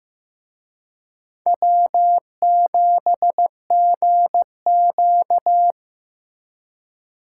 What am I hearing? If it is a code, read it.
Morse code, heard as a keyed tone, about 18 wpm, 700 Hz. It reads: W7GQ
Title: Speed Racing - USA Call Signs 15wpm